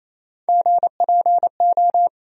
Morse code, heard as a keyed tone, about 28 wpm, 700 Hz. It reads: GPO
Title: USA Call Sign Postfixes - 3 Letters 28wpm